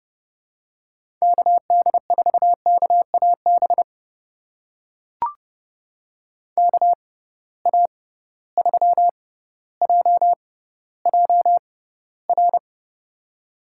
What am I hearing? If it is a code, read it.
Morse code, heard as a keyed tone, about 32 wpm, spaced as for 16 wpm, 700 Hz. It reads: KD4KAB KA3JJR
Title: USA Call Signs - 6 Characters - 6x Character Spacing - 30wpm